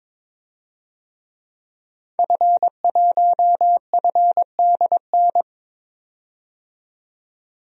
Morse code, heard as a keyed tone, about 22 wpm, 700 Hz. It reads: F1FDN
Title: DXCC-100 Call Signs - 5 Characters - 22wpm